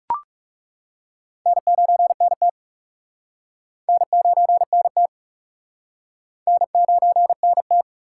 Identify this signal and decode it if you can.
Morse code, heard as a keyed tone, about 38 wpm, 700 Hz. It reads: N9NT N9NT N9NT
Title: Speed Racing - USA 1x2 Call Signs 30wpm